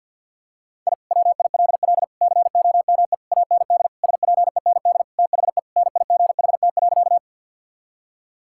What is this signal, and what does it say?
Morse code, heard as a keyed tone, about 48 wpm, 690 Hz. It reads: I WILL COME AND SPEND THE NIGHT.
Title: Sentences from Top 1000 Words - Review Part 5 - 50wpm